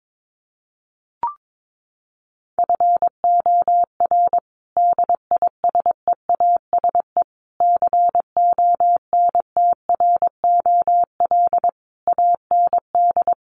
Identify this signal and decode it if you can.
Morse code, heard as a keyed tone, about 22 wpm, 700 Hz. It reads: FOR DISEASE CONTROL AND
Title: Sets of 5 Words Encore2 22wpm